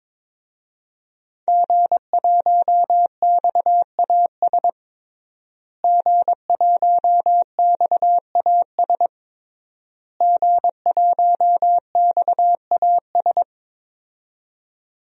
Morse code, heard as a keyed tone, about 22 wpm, 700 Hz. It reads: G1XAS G1XAS G1XAS
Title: DXCC-100 Call Signs - 5 Characters - Repeated 3x - 22wpm